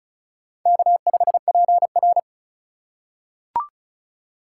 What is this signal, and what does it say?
Morse code, heard as a keyed tone, about 34 wpm, 700 Hz. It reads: K5PR
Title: USA Call Signs - 4 Characters - 2x Character Spacing - 35wpm